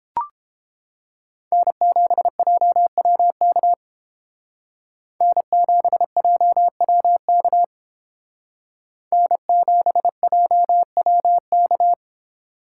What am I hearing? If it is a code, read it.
Morse code, heard as a keyed tone, about 31 wpm, 700 Hz. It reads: N7JWK N7JWK N7JWK